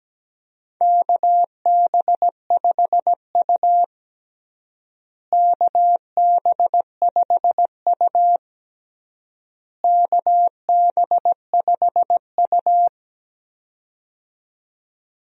Morse code, heard as a keyed tone, about 17 wpm, 700 Hz. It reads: KB5U KB5U KB5U
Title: USA Call Signs - 4 Characters - Repeated 3x - 17wpm